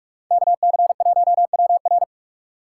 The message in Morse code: KC1WR

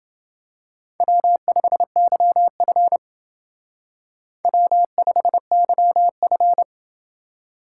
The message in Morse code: W5YF W5YF

